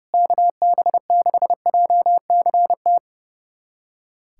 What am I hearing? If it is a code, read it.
Morse code, heard as a keyed tone, about 30 wpm, 700 Hz. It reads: KB6JCT
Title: USA Callsign DE Callsign - 6 Characters - 30wpm